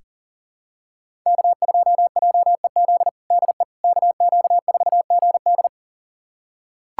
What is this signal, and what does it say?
Morse code, heard as a keyed tone, about 40 wpm, 700 Hz. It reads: K2JEZ DE KQ4GD
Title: USA Callsign DE Callsign - 5 Characters - 40wpm